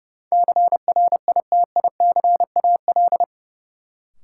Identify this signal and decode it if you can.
Morse code, heard as a keyed tone, about 30 wpm, 700 Hz. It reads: CRITICAL
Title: Sentences from Top 1100 Words - New Word Drill Alt - 30wpm